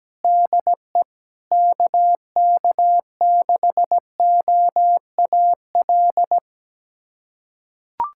Morse code, heard as a keyed tone, about 17 wpm, 700 Hz. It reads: DE KK6OAL
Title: USA Callsign DE Callsign - 6 Characters - 17wpm